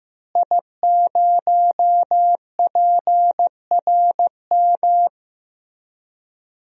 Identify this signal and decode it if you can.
Morse code, heard as a keyed tone, about 15 wpm, 700 Hz. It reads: I0PRM
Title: DXCC-100 Call Signs - 5 Characters - Repeated 3x - 15wpm